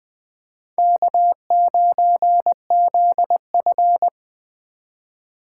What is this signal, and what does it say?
Morse code, heard as a keyed tone, about 22 wpm, 700 Hz. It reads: K9ZF